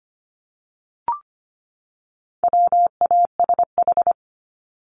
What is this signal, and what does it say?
Morse code, heard as a keyed tone, about 25 wpm, 700 Hz. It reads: WASH
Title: Top 1751-2000 Words 25wpm